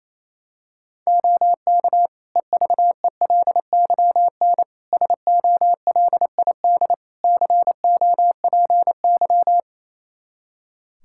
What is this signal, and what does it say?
Morse code, heard as a keyed tone, about 28 wpm, 700 Hz. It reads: OK EVELYN SOLID COPY
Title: Practice QSO - Exchange 3 - More Info - 28wpm